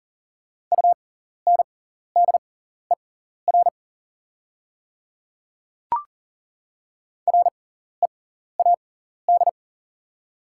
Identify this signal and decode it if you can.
Morse code, heard as a keyed tone, about 40 wpm, spaced as for 14 wpm, 690 Hz. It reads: UNDER READ
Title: 6x Character Spacing - Top 200 Words - 40wpm